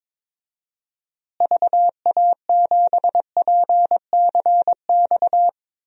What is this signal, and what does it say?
Morse code, heard as a keyed tone, about 22 wpm, 700 Hz. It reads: VA7PCX
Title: DXCC-100 Call Signs - 6 Characters - Repeated 3x - 22wpm